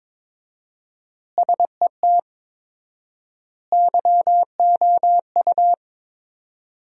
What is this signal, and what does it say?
Morse code, heard as a keyed tone, about 22 wpm, 700 Hz. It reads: SET YOU
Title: 4x Word-Spacing - Sentences from Top 200 Words - 22wpm